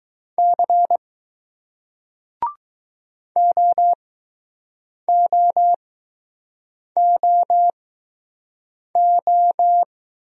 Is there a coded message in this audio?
Yes: Morse code from 0.4 to 9.8 s, heard as a keyed tone, about 20 wpm, spaced as for 6 wpm, 700 Hz. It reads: C OOOO